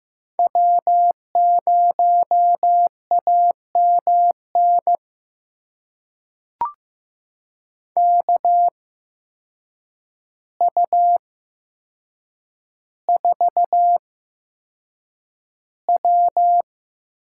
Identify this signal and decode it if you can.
Morse code, heard as a keyed tone, about 15 wpm, spaced as for 7 wpm, 700 Hz. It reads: W0AMN KU4W
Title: USA Call Signs - 5 Characters - 8x Character Spacing - 15wpm